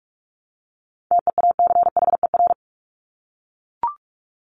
Morse code, heard as a keyed tone, about 44 wpm, 700 Hz. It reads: TEACHER
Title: Top 751-1000 Words 45wpm